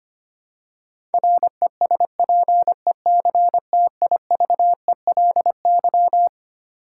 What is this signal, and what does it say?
Morse code, heard as a keyed tone, about 25 wpm, 700 Hz. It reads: RESPECTIVELY
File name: Twelve Letter Words - 25wpm